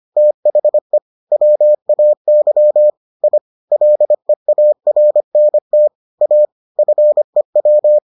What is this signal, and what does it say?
Morse code, heard as a keyed tone, about 25 wpm, 600 Hz. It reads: THE WAY I LEARNT A FEW